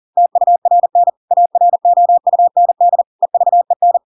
Morse code, heard as a keyed tone, about 40 wpm, 700 Hz. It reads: TURN AROUND EVEN
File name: Pinocchio / Pinocchio ch4 40 40 ik1hge